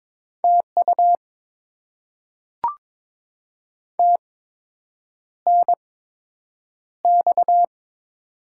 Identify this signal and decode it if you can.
Morse code, heard as a keyed tone, about 21 wpm, spaced as for 8 wpm, 700 Hz. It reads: TU TNX
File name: Top 20 QSO Elements - 8x Character Spacing - 22wpm